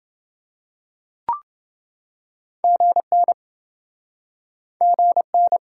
Morse code, heard as a keyed tone, about 28 wpm, 700 Hz. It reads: GN GN